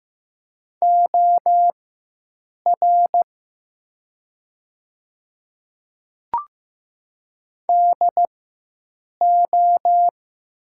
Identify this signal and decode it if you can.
Morse code, heard as a keyed tone, about 15 wpm, spaced as for 7 wpm, 700 Hz. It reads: OR DO
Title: 4x Character Spacing - Top 200 Words - 15wpm